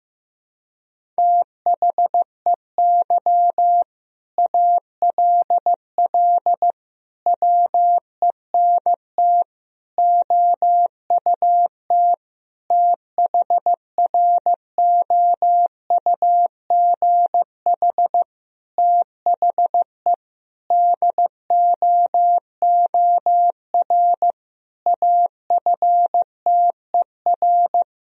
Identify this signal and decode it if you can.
Morse code, heard as a keyed tone, about 15 wpm, 700 Hz. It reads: THEY ALL WENT OUT THROUGH THE DOOR AFTER
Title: Sentences from Top 1000 Words - Review Part 1 - 15wpm